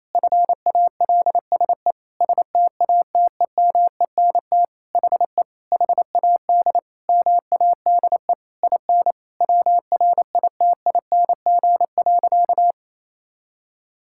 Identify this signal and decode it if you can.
Morse code, heard as a keyed tone, about 28 wpm, 700 Hz. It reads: FALSE STATEMENT HE HAD MADE IN WRITING.
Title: Sentences from Top 1600 Words 28wpm